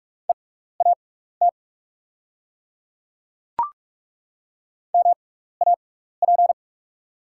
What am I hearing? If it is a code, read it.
Morse code, heard as a keyed tone, about 42 wpm, spaced as for 15 wpm, 700 Hz. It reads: EAT MAP
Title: 6x Character Spacing - Top 500 Words - 45wpm